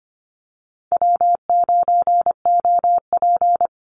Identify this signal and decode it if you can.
Morse code, heard as a keyed tone, about 25 wpm, 700 Hz. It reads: W9OP
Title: Call Signs - 25wpm